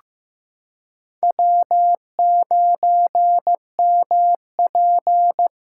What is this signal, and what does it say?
Morse code, heard as a keyed tone, about 15 wpm, 700 Hz. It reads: W9MP
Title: USA Callsign DE Callsign - 4 Characters - 15wpm